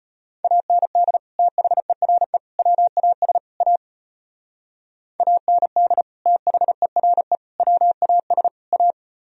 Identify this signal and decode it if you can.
Morse code, heard as a keyed tone, about 30 wpm, 700 Hz. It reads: AND THERE WAS A AND THERE WAS A
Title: Speed-Racing - Sets of 4 Words 25wpm